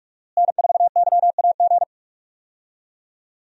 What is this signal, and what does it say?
Morse code, heard as a keyed tone, about 42 wpm, 700 Hz. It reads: N4YAG
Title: USA Call Signs - 5 Characters - Repeated 3x - 45wpm